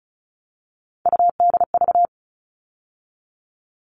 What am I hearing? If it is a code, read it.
Morse code, heard as a keyed tone, about 34 wpm, 700 Hz. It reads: UDV
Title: All Cut Numbers - 3-Digit 35wpm